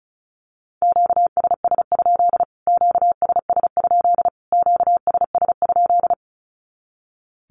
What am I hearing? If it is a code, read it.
Morse code, heard as a keyed tone, about 34 wpm, 700 Hz. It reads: QSS? QSS? QSS?
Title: Q-Codes Repeated 3x 35wpm